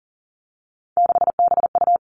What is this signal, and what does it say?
Morse code, heard as a keyed tone, about 40 wpm, 700 Hz. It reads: 6BU
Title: Three Challenging Letter-Number Combinations 40wpm